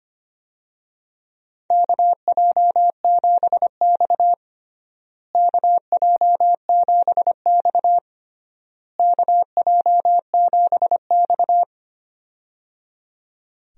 Morse code, heard as a keyed tone, about 25 wpm, 700 Hz. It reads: KJ7X KJ7X KJ7X
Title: USA Call Signs - 4 Characters - Repeated 3x - 25wpm